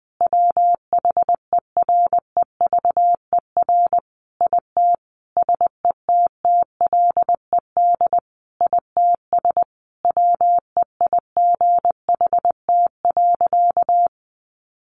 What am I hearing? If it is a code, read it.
Morse code, heard as a keyed tone, about 20 wpm, 700 Hz. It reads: WHEREVER IT SETTLED ITS WEIGHT.
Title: A Sound Of Thunder 20wpm